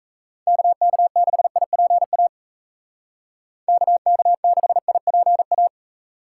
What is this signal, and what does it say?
Morse code, heard as a keyed tone, about 33 wpm, 700 Hz. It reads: KK6IPA KK6IPA
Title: Speed Racing - USA 2x3 Call Signs 28wpm